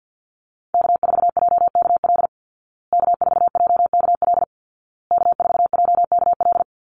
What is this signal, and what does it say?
Morse code, heard as a keyed tone, about 50 wpm, 700 Hz. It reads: K4JKL K4JKL K4JKL
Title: Call Signs Repeated 2x - 50wpm